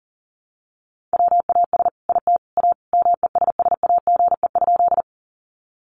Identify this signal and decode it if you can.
Morse code, heard as a keyed tone, about 40 wpm, 700 Hz. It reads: WAS IT A MESSAGE?
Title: Sentences from Top 1100 Words 40wpm